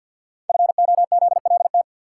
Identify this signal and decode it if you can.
Morse code, heard as a keyed tone, about 49 wpm, 700 Hz. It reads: FOZLT
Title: Code Groups - 5 Letters - 50wpm